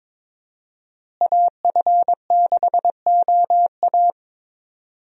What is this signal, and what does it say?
Morse code, heard as a keyed tone, about 22 wpm, 700 Hz. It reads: AF6OA
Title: USA Call Signs - 5 Characters - Repeated 2x - 22wpm